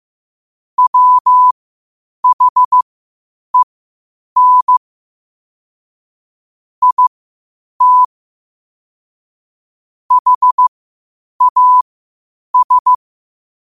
Morse code, heard as a keyed tone, about 15 wpm, spaced as for 8 wpm, 1000 Hz. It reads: WHEN IT HAS
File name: qotdfiles / 8wpm